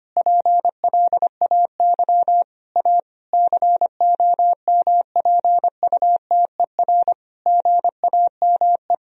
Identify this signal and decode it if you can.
Morse code, heard as a keyed tone, about 25 wpm, 700 Hz. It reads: PLAY A COMPUTER GAME
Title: Sentences from Top 1000 Words - Review Part 3 - 25wpm